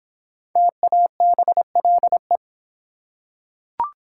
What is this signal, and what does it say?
Morse code, heard as a keyed tone, about 26 wpm, 700 Hz. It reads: TABLE